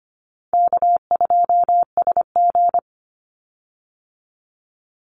Morse code, heard as a keyed tone, about 25 wpm, 700 Hz. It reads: K2SG
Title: Call Signs Repeated 3x - 25wpm